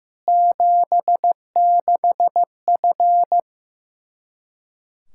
Morse code, heard as a keyed tone, about 15 wpm, 700 Hz. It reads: 76F
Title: Practice QSO - Exchange 2 - Info - 15wpm